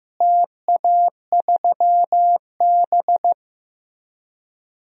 Morse code, heard as a keyed tone, about 15 wpm, 700 Hz. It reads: TA3B
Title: DXCC-100 Call Signs - 4 Characters - Repeated 3x - 15wpm